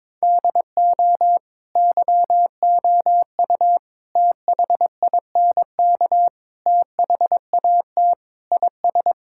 Morse code, heard as a keyed tone, about 22 wpm, 700 Hz. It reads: DO YOU THINK THAT IS
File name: practice / 22wpm / Sentences with 4-Word Phrases - 22wpm